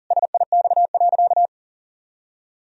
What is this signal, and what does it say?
Morse code, heard as a keyed tone, about 37 wpm, 700 Hz. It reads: SIX.
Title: Sentences from Top 1000 Words - Review Part 2 - 40wpm